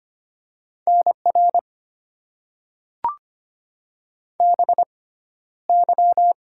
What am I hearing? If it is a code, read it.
Morse code, heard as a keyed tone, about 25 wpm, spaced as for 13 wpm, 700 Hz. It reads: NR BY
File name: Top 90 QSO Elements - 6x Character Spacing - 25wpm